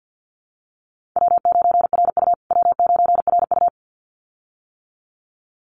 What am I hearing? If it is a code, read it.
Morse code, heard as a keyed tone, about 50 wpm, 700 Hz. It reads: W9RU W9RU
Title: USA Call Signs Repeated 2x 50wpm